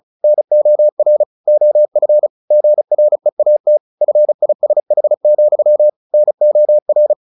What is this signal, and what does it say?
Morse code, heard as a keyed tone, about 34 wpm, 600 Hz. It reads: NOR OF GREAT FISH, NOR
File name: the old man and the sea / the old man and the sea 1 35wpm